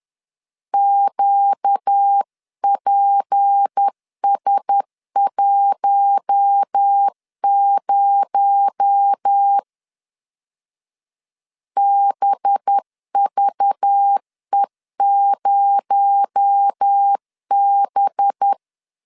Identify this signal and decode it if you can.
Morse code, heard as a keyed tone, about 10 wpm, 800 Hz. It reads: QPS10 BVE0B